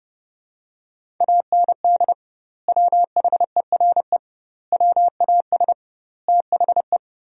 Morse code, heard as a keyed tone, about 32 wpm, 700 Hz. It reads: AND WHERE WAS THE
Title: 2x Word-Spacing - Sentences from Top 200 Words - 30wpm